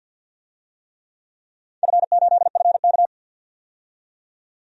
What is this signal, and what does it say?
Morse code, heard as a keyed tone, about 47 wpm, 700 Hz. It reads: F8FK
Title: DXCC-100 Call Signs - 4 Characters - 50wpm